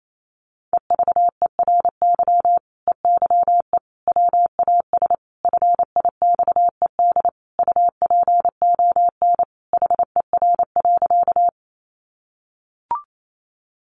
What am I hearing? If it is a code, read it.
Morse code, heard as a keyed tone, about 28 wpm, 700 Hz. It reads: EVERY EYE WAS FIXED UPON HER.
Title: Sentences from Top 1100 Words 28wpm